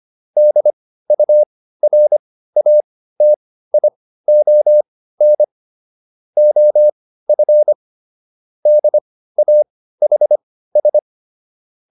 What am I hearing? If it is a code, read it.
Morse code, heard as a keyed tone, about 25 wpm, spaced as for 15 wpm, 600 Hz. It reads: DURATION OF DAHS